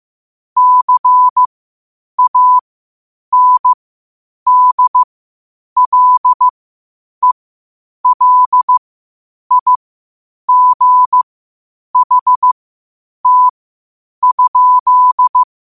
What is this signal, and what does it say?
Morse code, heard as a keyed tone, about 15 wpm, spaced as for 8 wpm, 1000 Hz. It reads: CANDLELIGHT?
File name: qotdfiles / 8wpm